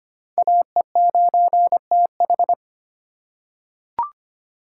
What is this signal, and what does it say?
Morse code, heard as a keyed tone, about 25 wpm, 700 Hz. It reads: AE9TH
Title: USA Call Signs - 4x Character Spacing - 25wpm